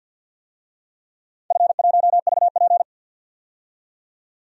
Morse code, heard as a keyed tone, about 50 wpm, 700 Hz. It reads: F1VP
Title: DXCC-100 Call Signs - 4 Characters - 50wpm